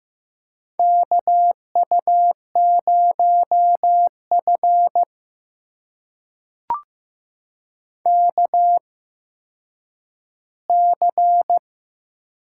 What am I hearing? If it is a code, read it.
Morse code, heard as a keyed tone, about 16 wpm, spaced as for 9 wpm, 700 Hz. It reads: KU0F KC
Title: USA Call Signs - 4 Characters - 8x Character Spacing - 15wpm